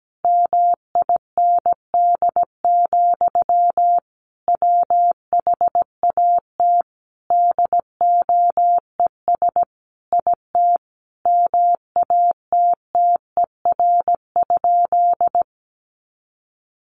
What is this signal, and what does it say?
Morse code, heard as a keyed tone, about 17 wpm, 700 Hz. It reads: MIND, WHAT DOES IT MATTER?
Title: Sentences from Top 700 Words 17wpm